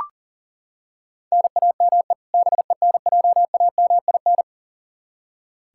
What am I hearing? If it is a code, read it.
Morse code, heard as a keyed tone, about 40 wpm, 700 Hz. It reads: NAME BENJAMIN
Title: K1USN SST QSO Practice 40wpm